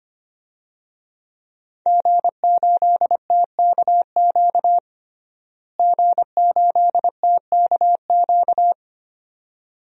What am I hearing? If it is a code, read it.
Morse code, heard as a keyed tone, about 25 wpm, 700 Hz. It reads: G8TKQ G8TKQ
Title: DXCC-100 Call Signs - 5 Characters - Repeated 2x - 25wpm